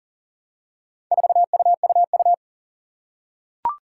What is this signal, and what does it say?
Morse code, heard as a keyed tone, about 37 wpm, 700 Hz. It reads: 4UUU
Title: Four Challenging Letter-Number Combinations 40wpm